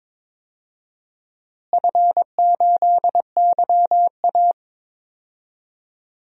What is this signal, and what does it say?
Morse code, heard as a keyed tone, about 22 wpm, 700 Hz. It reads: F8YA
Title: DXCC-100 Call Signs - 4 Characters - Repeated 2x - 22wpm